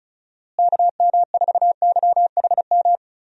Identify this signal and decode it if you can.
Morse code, heard as a keyed tone, about 36 wpm, 700 Hz. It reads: KM4YHM
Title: USA Call Signs - 6 Characters - Repeated 3x - 35wpm